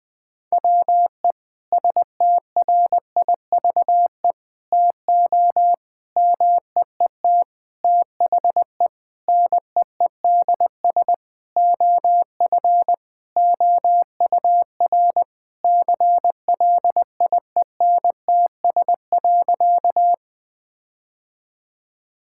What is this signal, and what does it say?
Morse code, heard as a keyed tone, about 20 wpm, 700 Hz. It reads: WE STRIVE TO MEET THE NEEDS OF OUR CLIENTS.
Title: Sentences with 5-Word Phrases - 20wpm